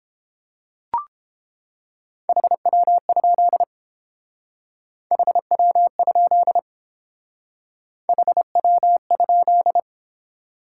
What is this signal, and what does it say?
Morse code, heard as a keyed tone, about 30 wpm, 700 Hz. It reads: HW? HW? HW?